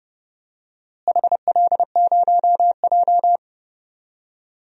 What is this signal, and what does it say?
Morse code, heard as a keyed tone, about 32 wpm, 700 Hz. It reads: HL0J
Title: DXCC-100 Call Signs - 4 Characters - Repeated 3x - 30wpm